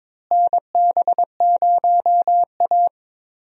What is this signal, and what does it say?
Morse code, heard as a keyed tone, about 22 wpm, 700 Hz. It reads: NB0A
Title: USA Call Signs - 4 Characters - Repeated 3x - 22wpm